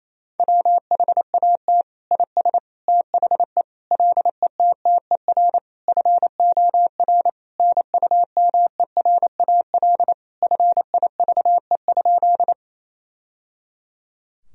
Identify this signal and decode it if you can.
Morse code, heard as a keyed tone, about 28 wpm, 700 Hz. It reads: WHAT IS THE LETTER FOR NUMERAL FIVE?
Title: Sentences from Top 1000 Words - Review Part 2 - 28wpm